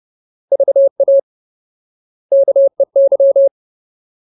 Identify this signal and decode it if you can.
Morse code, heard as a keyed tone, about 30 wpm, 550 Hz. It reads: VA KEY
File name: Ultimate Word List / 30WPM Ultimate